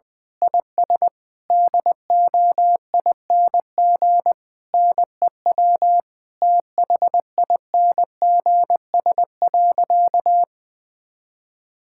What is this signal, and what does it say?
Morse code, heard as a keyed tone, about 21 wpm, 700 Hz. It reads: IS DOING NEW THINGS.
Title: Difficult Sentences with Hints - 20wpm